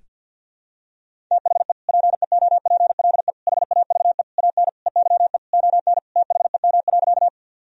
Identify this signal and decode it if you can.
Morse code, heard as a keyed tone, about 52 wpm, 700 Hz. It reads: THE PEOPLE HAVE AN EYE ON THEM.
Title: Sentences from Top 1000 Words - Review Part 1 - 50wpm